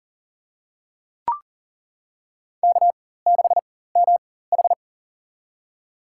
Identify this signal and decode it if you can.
Morse code, heard as a keyed tone, about 40 wpm, spaced as for 19 wpm, 700 Hz. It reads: K6MH
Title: USA Call Signs - 4 Characters - 4x Character Spacing - 40wpm